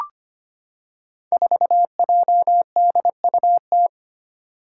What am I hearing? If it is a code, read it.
Morse code, heard as a keyed tone, about 25 wpm, 700 Hz. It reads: 4JDUT
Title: Five Challenging Letter-Number Combinations 25wpm